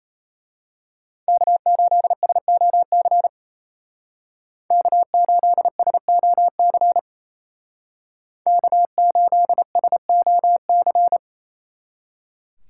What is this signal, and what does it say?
Morse code, heard as a keyed tone, about 34 wpm, 700 Hz. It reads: K8SOC K8SOC K8SOC
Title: Speed Racing - USA Call Signs 28wpm